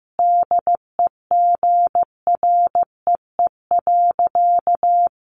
Morse code, heard as a keyed tone, about 15 wpm, 700 Hz. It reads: DEGREE.
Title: Sentences from Top 1500 Words 15wpm